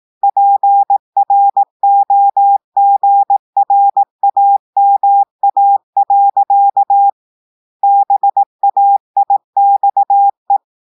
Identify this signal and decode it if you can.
Morse code, heard as a keyed tone, about 18 wpm, 800 Hz. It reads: PROGRAMA. BAIXE
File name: audios / portugais / portugais texte en clair 2 vitesse 18 mots minutes